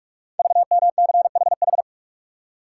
Morse code, heard as a keyed tone, about 44 wpm, 700 Hz. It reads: VMCHH
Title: Code Groups - 5 Letter:Numbers - 45wpm